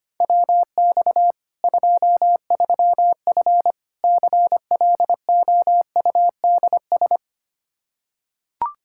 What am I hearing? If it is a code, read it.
Morse code, heard as a keyed tone, about 25 wpm, 700 Hz. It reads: WX 23F CLOUDS
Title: Practice QSO - Exchange 2 - Info - 25wpm